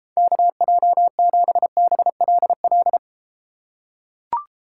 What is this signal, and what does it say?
Morse code, heard as a keyed tone, about 32 wpm, 700 Hz. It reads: KJ7BLL